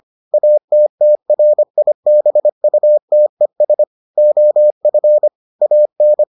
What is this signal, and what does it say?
Morse code, heard as a keyed tone, about 25 wpm, 600 Hz. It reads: ATTRIBUTES OF AN